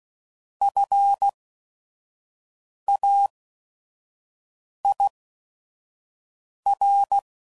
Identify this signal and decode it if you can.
Morse code, heard as a keyed tone, about 16 wpm, spaced as for 5 wpm, 800 Hz. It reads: FAIR
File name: Prefixes / By First Letter / Fs